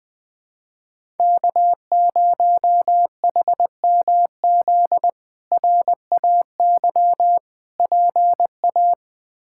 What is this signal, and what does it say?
Morse code, heard as a keyed tone, about 20 wpm, 700 Hz. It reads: K0HMZ RAY PA
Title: K1USN SST QSO Practice - 20wpm